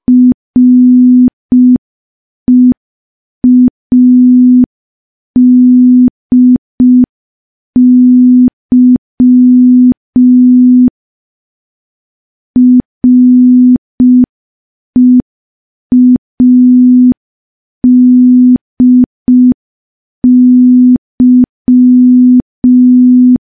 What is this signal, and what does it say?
Morse code, heard as a keyed tone, about 5 wpm, 250 Hz. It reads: READY READY
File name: morse / morse-training-05WPM